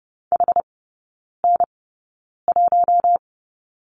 Morse code, heard as a keyed tone, about 30 wpm, spaced as for 9 wpm, 700 Hz. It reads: HN1
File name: 081 Lesson 14e - Number 2 - Three Characters 30wpm